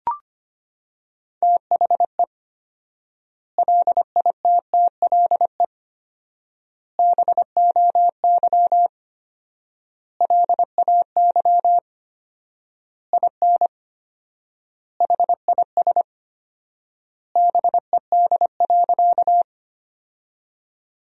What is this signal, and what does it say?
Morse code, heard as a keyed tone, about 25 wpm, 700 Hz. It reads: THE LITTLE BOY LAY IN HIS BED.
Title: 4x Word-Spacing - Sentences from Top 500 Words - 25wpm